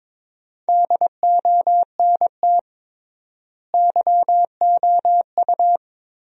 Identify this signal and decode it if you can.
Morse code, heard as a keyed tone, about 22 wpm, 700 Hz. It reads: DONT YOU
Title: 3x Word-Spacing - Sentences from Top 300 Words - 22wpm